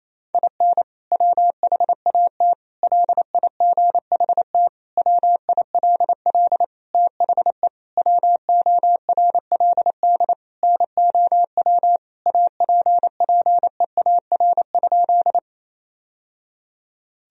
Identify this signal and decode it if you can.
Morse code, heard as a keyed tone, about 28 wpm, 700 Hz. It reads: IN WHAT LIGHT WILL THE WORLD NOW APPEAR?
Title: Sentences from Top 1000 Words - Review Part 2 - 28wpm